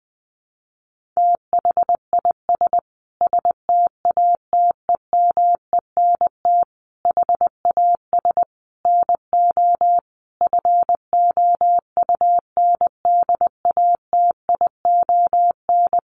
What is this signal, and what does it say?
Morse code, heard as a keyed tone, about 20 wpm, 700 Hz. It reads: THIS STATEMENT HAS NO FOUNDATION
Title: Sentences from Top 1200 Words 20wpm